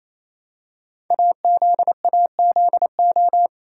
Morse code, heard as a keyed tone, about 27 wpm, 700 Hz. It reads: AZAZO
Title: Code Groups - 5 Letters - 28wpm